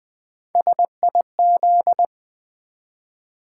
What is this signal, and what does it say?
Morse code, heard as a keyed tone, about 20 wpm, 700 Hz. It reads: SIZ
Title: Build A Word - 1 Letter At A Time - 4-Letter Words - 20wpm